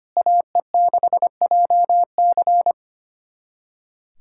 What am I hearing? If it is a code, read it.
Morse code, heard as a keyed tone, about 25 wpm, 700 Hz. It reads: AE6JC
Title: USA Callsign DE Callsign - 5 Characters - 25wpm